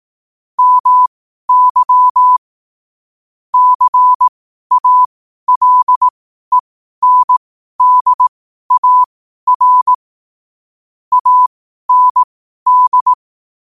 Morse code, heard as a keyed tone, about 18 wpm, spaced as for 12 wpm, 1000 Hz. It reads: MY CALENDAR AND
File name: qotdfiles / 12wpm